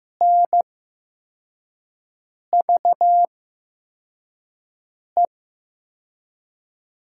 Morse code, heard as a keyed tone, about 15 wpm, spaced as for 4 wpm, 700 Hz. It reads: NVE
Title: USA Call Signs - 6 Characters - 8x Character Spacing - 15wpm